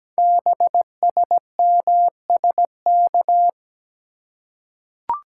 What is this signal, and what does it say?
Morse code, heard as a keyed tone, about 17 wpm, 700 Hz. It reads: BSMSK
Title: Code Groups - 5 Letters - 17wpm